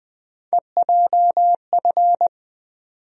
Morse code, heard as a keyed tone, about 20 wpm, 700 Hz. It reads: EJF
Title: USA Call Sign Postfixes - 3 Letters 20wpm